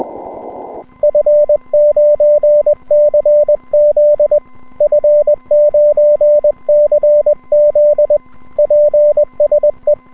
Morse code, heard as a keyed tone, about 21 wpm, 590 Hz, with radio noise. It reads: F9CZ F9CZ PSE
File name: n6tr / N6TR-7deHA5JIon160M-06-Sep- 31UTC